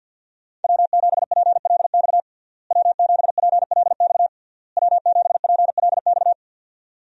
Morse code, heard as a keyed tone, about 52 wpm, 700 Hz. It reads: W7PLX W7PLX W7PLX
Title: USA Call Signs - 5 Characters - Repeated 3x - 50wpm